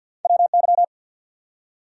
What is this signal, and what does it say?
Morse code, heard as a keyed tone, about 48 wpm, 700 Hz. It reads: WY
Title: US States and Canadian Provinces 50wpm